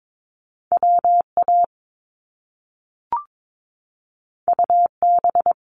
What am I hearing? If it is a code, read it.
Morse code, heard as a keyed tone, about 22 wpm, 700 Hz. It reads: WA UB
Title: Top Two Letter Combinations in Words 22wpm